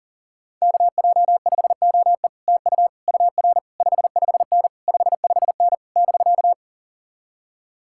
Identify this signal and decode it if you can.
Morse code, heard as a keyed tone, about 44 wpm, 700 Hz. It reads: KJ5OE TU UR 55N 55N <BK>